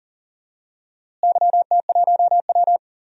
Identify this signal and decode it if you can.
Morse code, heard as a keyed tone, about 40 wpm, 700 Hz. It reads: YT1W
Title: DXCC-100 Call Signs - 4 Characters - Repeated 2x - 40wpm